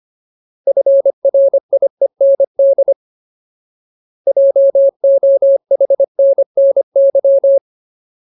Morse code, heard as a keyed tone, about 25 wpm, 550 Hz. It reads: FRIEND JOHNNY
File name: Ultimate Word List / 25WPM Ultimate